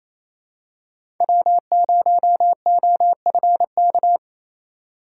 Morse code, heard as a keyed tone, about 28 wpm, 700 Hz. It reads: W0OFK
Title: USA Call Signs - 5 Characters - 2x Character Spacing - 28wpm